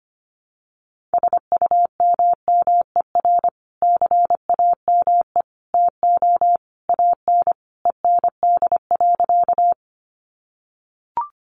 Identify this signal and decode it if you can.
Morse code, heard as a keyed tone, about 25 wpm, 700 Hz. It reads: SUMMER CAME TO AN END.
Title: Sentences from Top 1100 Words 25wpm